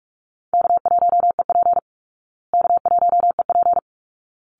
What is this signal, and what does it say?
Morse code, heard as a keyed tone, about 44 wpm, 700 Hz. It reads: K1EP K1EP
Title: Call Signs Repeated 3x - 45wpm